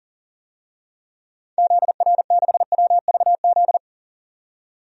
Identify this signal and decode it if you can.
Morse code, heard as a keyed tone, about 40 wpm, 700 Hz. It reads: ZR6WVZ
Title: DXCC-100 Call Signs - 6 Characters - 40wpm